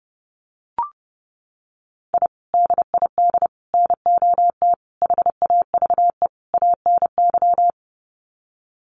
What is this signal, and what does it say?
Morse code, heard as a keyed tone, about 30 wpm, 700 Hz. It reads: I DID NOT HAVE ANY
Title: Sets of 5 Words Encore 30wpm